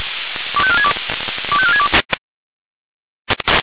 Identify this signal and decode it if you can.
Five-tone selective call (ZVEI), heard as quick runs of five tones, tones of about 70 ms. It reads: 25462 25462
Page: /SDR/ZVEI/